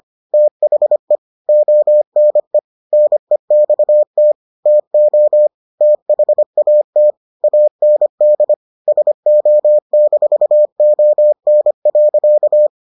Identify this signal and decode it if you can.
Morse code, heard as a keyed tone, about 25 wpm, 600 Hz. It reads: THE ONE NEXT TO THAT AND SO-ON.